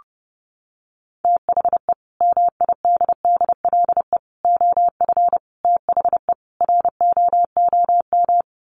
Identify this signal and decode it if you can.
Morse code, heard as a keyed tone, about 28 wpm, 700 Hz. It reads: THE MIDDLE OF THE ROOM
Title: Sets of 5 Words Encore2 30wpm